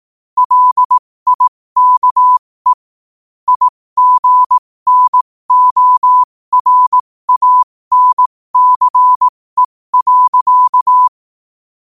Morse code, heard as a keyed tone, about 18 wpm, spaced as for 15 wpm, 1000 Hz. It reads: LIKE IGNORANCE.
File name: qotdfiles / 15wpm